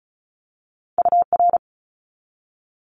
Morse code, heard as a keyed tone, about 34 wpm, 700 Hz. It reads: UR
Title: Abbreviations 35wpm